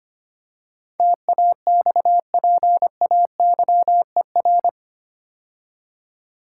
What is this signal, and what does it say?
Morse code, heard as a keyed tone, about 25 wpm, 700 Hz. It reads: TAXPAYER
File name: 8 Letter Words - First 3 Letters Given - 25wpm